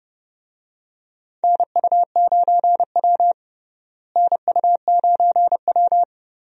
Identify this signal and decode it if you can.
Morse code, heard as a keyed tone, about 31 wpm, 700 Hz. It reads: NU9W NU9W
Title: USA Call Signs - 4 Characters - Repeated 3x - 30wpm